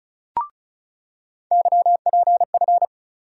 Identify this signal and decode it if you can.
Morse code, heard as a keyed tone, about 34 wpm, 700 Hz. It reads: YPF
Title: USA Call Sign Postfixes - 3 Letters 35wpm